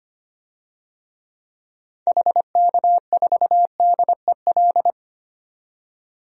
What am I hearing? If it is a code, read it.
Morse code, heard as a keyed tone, about 25 wpm, 700 Hz. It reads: HK4DEL
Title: DXCC-100 Call Signs - 6 Characters - 25wpm